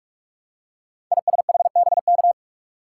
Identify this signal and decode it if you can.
Morse code, heard as a keyed tone, about 44 wpm, 700 Hz. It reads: ISHBK
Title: Code Groups - 5 Letters - 45wpm